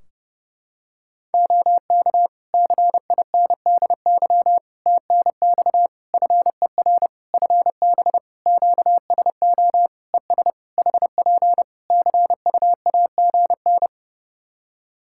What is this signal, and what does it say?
Morse code, heard as a keyed tone, about 32 wpm, 700 Hz. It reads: OK CINDY TNX FER FB QSO ES HP CUAGN
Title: K1USN SST QSO Practice 30wpm